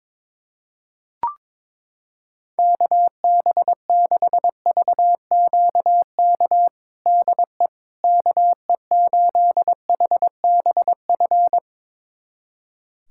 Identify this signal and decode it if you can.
Morse code, heard as a keyed tone, about 22 wpm, 700 Hz. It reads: KB6VQK DE KE8HBF
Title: USA Callsign DE Callsign - 6 Characters - 22wpm